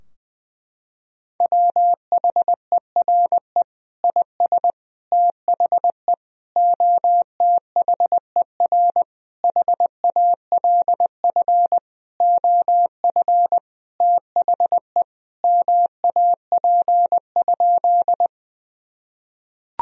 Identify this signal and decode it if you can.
Morse code, heard as a keyed tone, about 20 wpm, 700 Hz. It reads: WHERE IS THE OTHER HALF OF THE MAP?
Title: Sentences from Top 1000 Words - Review Part 2 - 20wpm